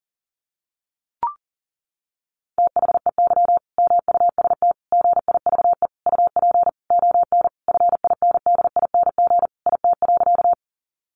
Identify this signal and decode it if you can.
Morse code, heard as a keyed tone, about 40 wpm, 700 Hz. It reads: THEY MUST GIVE UP ON FINDING IT.
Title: Sentences from Top 1100 Words 40wpm